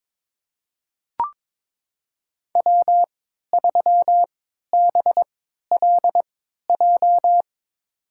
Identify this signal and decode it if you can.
Morse code, heard as a keyed tone, about 22 wpm, spaced as for 13 wpm, 700 Hz. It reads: W3BLJ
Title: USA Call Signs - 3x Character Spacing - 22wpm